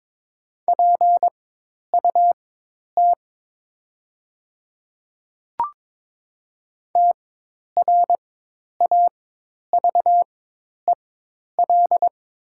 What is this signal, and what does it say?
Morse code, heard as a keyed tone, about 23 wpm, spaced as for 10 wpm, 700 Hz. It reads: PUT TRAVEL
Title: 4x Character Spacing - Top 500 Words - 22wpm